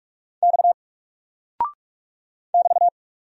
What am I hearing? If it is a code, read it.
Morse code, heard as a keyed tone, about 44 wpm, 700 Hz. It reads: X <BT>
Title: Single Character (Letter:Number:Punctuation:Prosign) 45wpm